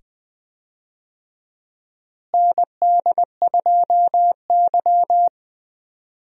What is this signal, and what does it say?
Morse code, heard as a keyed tone, about 20 wpm, 700 Hz. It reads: ND2Y
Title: USA Call Signs - 4 Characters - Repeated 2x - 20wpm